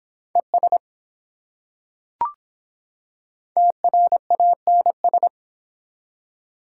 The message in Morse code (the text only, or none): ES TRANS